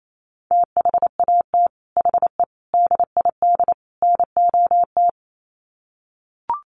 Morse code, heard as a keyed tone, about 28 wpm, 700 Hz. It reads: THAT HE DID NOT
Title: Sets of 4 Words 28wpm